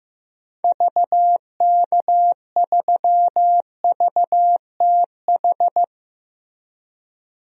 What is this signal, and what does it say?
Morse code, heard as a keyed tone, about 15 wpm, 700 Hz. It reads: VK3VTH
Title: DXCC-100 Call Signs - 6 Characters - 15wpm